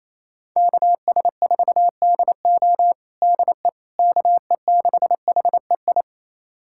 Morse code, heard as a keyed tone, about 28 wpm, 700 Hz. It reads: KS4DO DE KE6HEI
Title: USA Callsign DE Callsign 28wpm